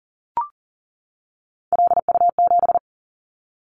Morse code, heard as a keyed tone, about 38 wpm, 700 Hz. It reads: LU7
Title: Callsign Prefixes for Top 50 DXCC - 40wpm